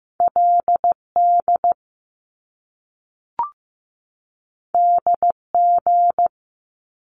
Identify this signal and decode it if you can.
Morse code, heard as a keyed tone, about 15 wpm, 700 Hz. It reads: LD DG
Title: Top Two Letter Combinations in Words 15wpm